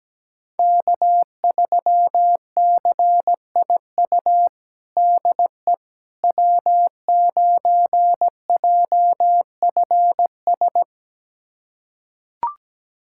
Morse code, heard as a keyed tone, about 17 wpm, 700 Hz. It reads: K3CIU DE W9JFS
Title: USA Callsign DE Callsign - 5 Characters - 17wpm